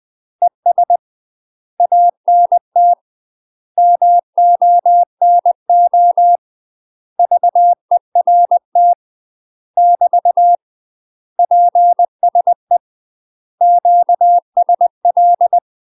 Morse code, heard as a keyed tone, about 20 wpm, 700 Hz. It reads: ES ANT MONO VERT <BT> PSE QSL
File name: UK MASTER QSOs TRAINING / 20WPM / UK MASTER QSOs 0058